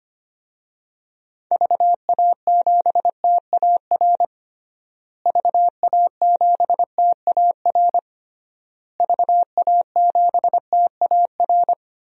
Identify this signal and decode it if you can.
Morse code, heard as a keyed tone, about 25 wpm, 700 Hz. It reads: VA7TAR VA7TAR VA7TAR
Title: DXCC-100 Call Signs - 6 Characters - Repeated 3x - 25wpm